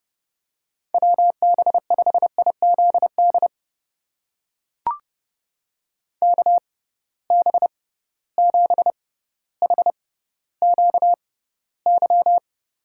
Morse code, heard as a keyed tone, about 30 wpm, spaced as for 15 wpm, 700 Hz. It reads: WB5IZD KB7HQY
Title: USA Call Signs - 6 Characters - 6x Character Spacing - 30wpm